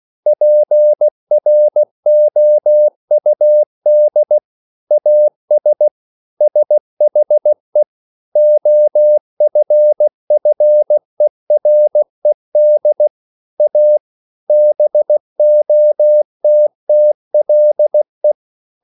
Morse code, heard as a keyed tone, about 16 wpm, 600 Hz. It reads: PROUD AS SHE OFFERED A BOTTLE